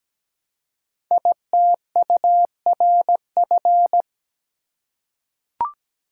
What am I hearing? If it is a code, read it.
Morse code, heard as a keyed tone, about 17 wpm, 700 Hz. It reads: ITURF
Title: Code Groups - 5 Letters - 17wpm